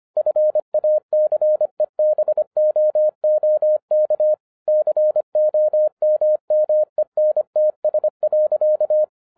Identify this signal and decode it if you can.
Morse code, heard as a keyed tone, about 25 wpm, 610 Hz. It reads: FACEBOOK COMMENTS.
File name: 2011-07-mp3 / n34 THE 3.11 2 25wpm